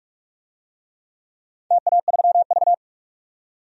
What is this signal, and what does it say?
Morse code, heard as a keyed tone, about 44 wpm, 700 Hz. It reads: TA3V
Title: DXCC-100 Call Signs - 4 Characters - 45wpm